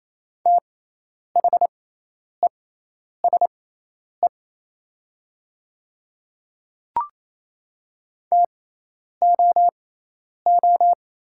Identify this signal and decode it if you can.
Morse code, heard as a keyed tone, about 28 wpm, spaced as for 10 wpm, 700 Hz. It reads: THESE TOO
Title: 6x Character Spacing - Top 500 Words - 28wpm